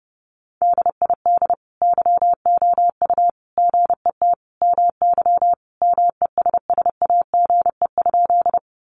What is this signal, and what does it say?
Morse code, heard as a keyed tone, about 28 wpm, 700 Hz. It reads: DID YOU GET MY MESSAGE?
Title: Sentences from Top 1100 Words 30wpm